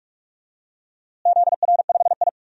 Morse code, heard as a keyed tone, about 46 wpm, 700 Hz. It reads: ZR5I
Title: DXCC-100 Call Signs - 4 Characters - Repeated 2x - 45wpm